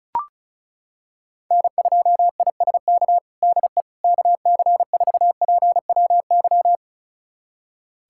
Morse code, heard as a keyed tone, about 36 wpm, 700 Hz. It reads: N2ISK DE KC4PWY
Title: USA Callsign DE Callsign 35wpm